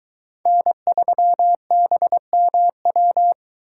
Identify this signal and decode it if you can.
Morse code, heard as a keyed tone, about 23 wpm, 700 Hz. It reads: N3BMW